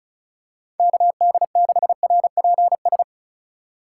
Morse code, heard as a keyed tone, about 34 wpm, 700 Hz. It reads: KD6RPS
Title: USA Call Signs - 6 Characters - 4x Character Spacing - 35wpm